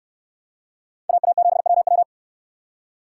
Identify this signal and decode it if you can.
Morse code, heard as a keyed tone, about 69 wpm, 700 Hz. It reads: AA7WW